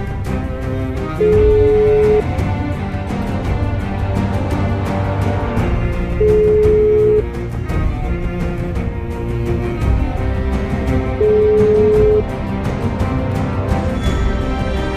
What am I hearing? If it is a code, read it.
Ringback tone (Europe, 425 Hz), with noise.